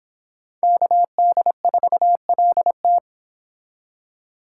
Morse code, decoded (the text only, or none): KD4LT